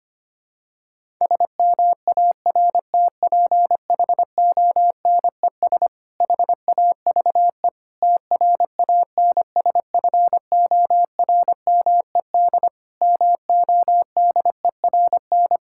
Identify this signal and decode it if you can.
Morse code, heard as a keyed tone, about 25 wpm, 700 Hz. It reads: SMARTPHONES HAVE TRANSFORMED MODERN
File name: practice / 25wpm / Difficult Sentences with Hints - 25wpm